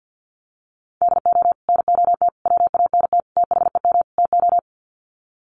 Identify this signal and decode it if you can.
Morse code, heard as a keyed tone, about 50 wpm, 700 Hz. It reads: DO NOT WANT THEM TO
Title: Sets of 5 Words Encore 50wpm